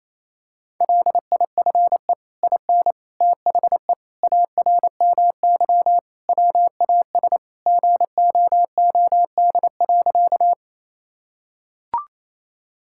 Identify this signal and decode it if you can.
Morse code, heard as a keyed tone, about 27 wpm, 700 Hz. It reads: LIFE IN THE ARMY WAS GOOD.
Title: Sentences from Top 1000 Words - Review Part 5 - 28wpm